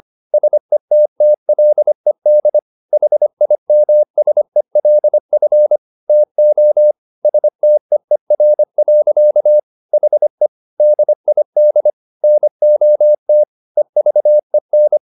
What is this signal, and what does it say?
Morse code, heard as a keyed tone, about 25 wpm, 600 Hz. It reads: SETTLED HIMSELF TO STEER. HE DID NOT EVEN